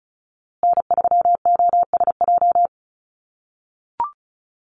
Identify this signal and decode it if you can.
Morse code, heard as a keyed tone, about 34 wpm, 700 Hz. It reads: N3OSJ
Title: USA Call Signs Repeated 2x 35wpm